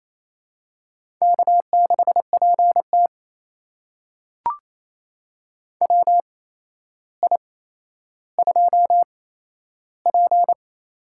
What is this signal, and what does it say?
Morse code, heard as a keyed tone, about 28 wpm, spaced as for 12 wpm, 700 Hz. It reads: K6PT WI2P
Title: USA Call Signs - 4 Characters - 8x Character Spacing - 28wpm